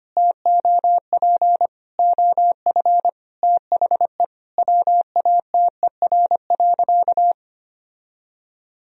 Morse code, heard as a keyed tone, about 25 wpm, 700 Hz. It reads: TOP OF THE WATER.
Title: Sentences from Top 1000 Words - Review Part 2 - 25wpm